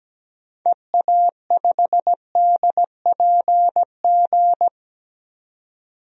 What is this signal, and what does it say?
Morse code, heard as a keyed tone, about 17 wpm, 700 Hz. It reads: EA5DPG
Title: DXCC-100 Call Signs - 6 Characters - 17wpm